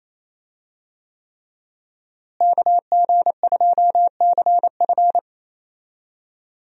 Morse code, heard as a keyed tone, about 28 wpm, 700 Hz. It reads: KG2CF
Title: USA Call Signs - 5 Characters - Repeated 3x - 28wpm